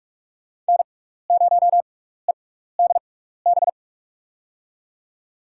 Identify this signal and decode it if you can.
Morse code, heard as a keyed tone, about 45 wpm, spaced as for 15 wpm, 700 Hz. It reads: N0EDB
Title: USA Call Signs - 5 Characters - 6x Character Spacing - 45wpm